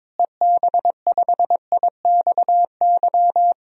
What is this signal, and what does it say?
Morse code, heard as a keyed tone, about 22 wpm, 700 Hz. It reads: EB5IXY
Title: DXCC-100 Call Signs - 6 Characters - Repeated 3x - 22wpm